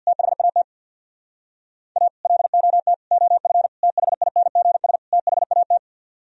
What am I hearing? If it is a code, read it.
Morse code, heard as a keyed tone, about 64 wpm, 700 Hz. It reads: THAT A LOT OF THINGS THAT